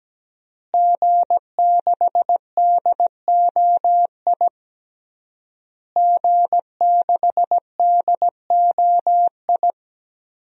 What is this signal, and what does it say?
Morse code, heard as a keyed tone, about 17 wpm, 700 Hz. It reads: G6DOI G6DOI
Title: DXCC-100 Call Signs - 5 Characters - Repeated 3x - 17wpm